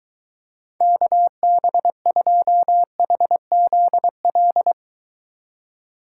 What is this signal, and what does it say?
Morse code, heard as a keyed tone, about 24 wpm, 700 Hz. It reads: KB2HZL